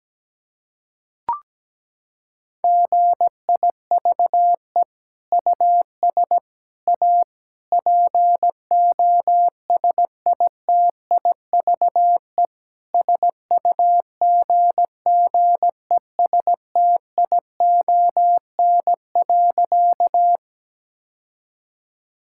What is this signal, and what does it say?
Morse code, heard as a keyed tone, about 17 wpm, 700 Hz. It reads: GIVE US A POSITIVE SUGGESTION.
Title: Sentences from Top 1600 Words 17wpm